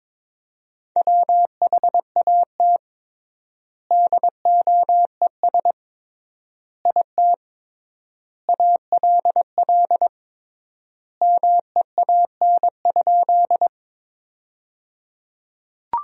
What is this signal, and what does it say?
Morse code, heard as a keyed tone, about 22 wpm, 700 Hz. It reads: WHAT DOES IT ALL MEAN?
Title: 3x Word-Spacing - Sentences from Top 200 Words - 22wpm